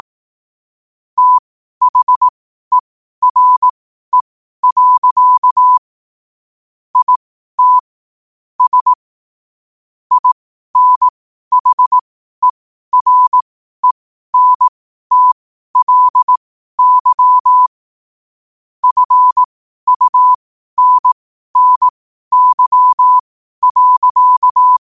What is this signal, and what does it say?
Morse code, heard as a keyed tone, about 18 wpm, spaced as for 12 wpm, 1000 Hz. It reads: THERE. IT S INHERENTLY FUNNY.
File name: qotdfiles / 12wpm